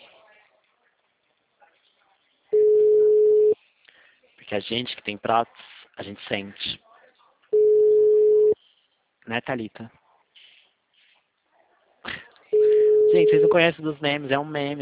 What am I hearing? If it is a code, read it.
Ringback tone (Europe, 425 Hz), with speech.